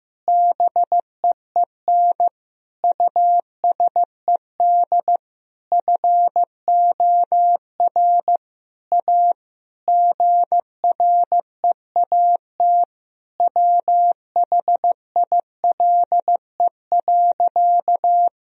Morse code, heard as a keyed tone, about 15 wpm, 700 Hz. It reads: BEEN USED FOR A GREAT WHILE.
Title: Sentences from Top 1000 Words - Review Part 1 - 15wpm